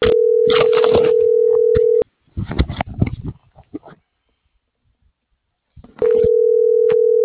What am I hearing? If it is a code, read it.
Ringback tone (North America), 440 + 480 Hz.